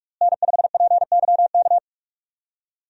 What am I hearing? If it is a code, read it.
Morse code, heard as a keyed tone, about 43 wpm, 700 Hz. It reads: N5PYK
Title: USA Call Signs - 5 Characters - Repeated 2x - 45wpm